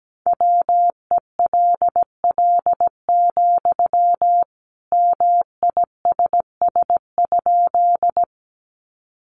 Morse code, heard as a keyed tone, about 17 wpm, 700 Hz. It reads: WELL, MISS?
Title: Sentences from Top 500 Words 17wpm